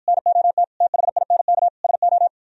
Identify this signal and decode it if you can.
Morse code, heard as a keyed tone, about 52 wpm, 700 Hz. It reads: NOT THINK SO